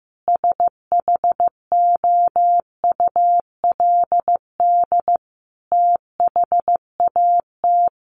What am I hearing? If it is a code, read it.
Morse code, heard as a keyed tone, about 15 wpm, 700 Hz. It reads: SHOULD THAT
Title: Sentences from Top 1300 Words 15wpm